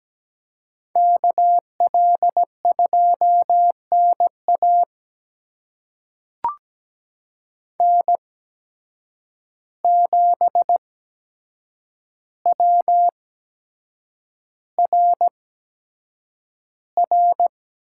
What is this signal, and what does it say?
Morse code, heard as a keyed tone, about 17 wpm, spaced as for 7 wpm, 700 Hz. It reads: KL2NA N7WRR
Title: USA Call Signs - 5 Characters - 8x Character Spacing - 17wpm